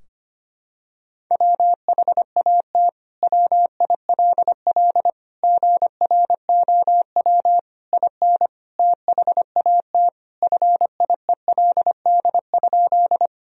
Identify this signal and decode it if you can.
Morse code, heard as a keyed tone, about 25 wpm, 700 Hz. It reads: WHAT WILL GROW IN THAT FIELD?
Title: Sentences from Top 1000 Words - Review Part 2 - 25wpm